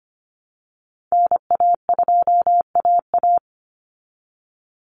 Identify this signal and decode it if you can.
Morse code, heard as a keyed tone, about 25 wpm, 700 Hz. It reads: NA2AA
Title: Call Signs Repeated 2x - 25wpm